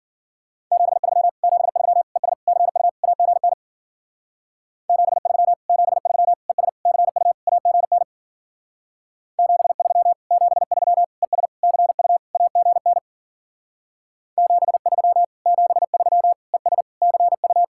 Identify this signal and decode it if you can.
Morse code, heard as a keyed tone, about 50 wpm, 700 Hz. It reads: 73 73 ES CU AGN 73 73 ES CU AGN 73 73 ES CU AGN 73 73 ES CU
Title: Speed-Racing Common QSO Phrases - 40wpm